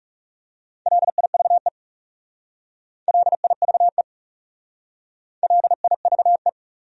Morse code, heard as a keyed tone, about 36 wpm, 700 Hz. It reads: LIVE LIVE LIVE
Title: Speed-Racing Top 200 Words 30wpm